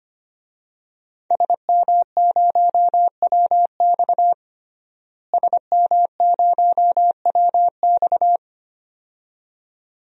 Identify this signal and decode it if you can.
Morse code, heard as a keyed tone, about 25 wpm, 700 Hz. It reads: SM0WX SM0WX
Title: DXCC-100 Call Signs - 5 Characters - Repeated 2x - 25wpm